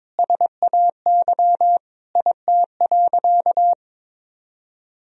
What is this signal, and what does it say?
Morse code, heard as a keyed tone, about 22 wpm, 700 Hz. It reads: SAY IT.
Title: Sentences from Top 400 Words and Abbreviations 22wpm